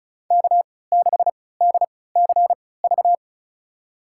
Morse code, heard as a keyed tone, about 36 wpm, spaced as for 20 wpm, 700 Hz. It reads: K6DCV
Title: USA Call Signs - 3x Character Spacing - 35wpm